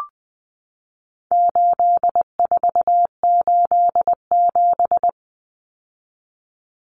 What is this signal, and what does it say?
Morse code, heard as a keyed tone, about 20 wpm, 700 Hz. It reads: 8487
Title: Common Cut Numbers - 4-Digit 20wpm